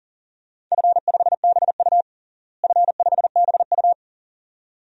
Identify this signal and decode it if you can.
Morse code, heard as a keyed tone, about 40 wpm, 700 Hz. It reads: F5BU F5BU
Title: DXCC-100 Call Signs - 4 Characters - Repeated 2x - 40wpm